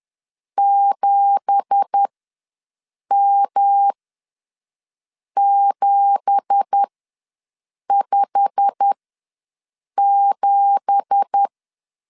Morse code, heard as a keyed tone, about 10 wpm, spaced as for 6 wpm, 800 Hz. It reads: 7M757
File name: LEZIONI / 9 2867 ws 10wpm 4215